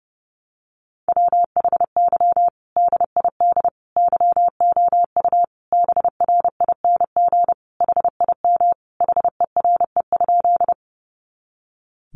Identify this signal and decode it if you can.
Morse code, heard as a keyed tone, about 30 wpm, 700 Hz. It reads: WHY DID YOU BRING HIM HERE?
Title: Sentences from Top 500 Words 30wpm